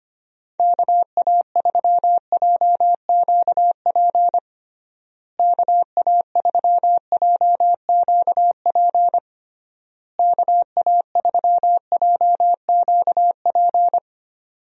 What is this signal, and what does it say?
Morse code, heard as a keyed tone, about 25 wpm, 700 Hz. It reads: KA3JQP KA3JQP KA3JQP
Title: USA Call Signs - 6 Characters - Repeated 3x - 25wpm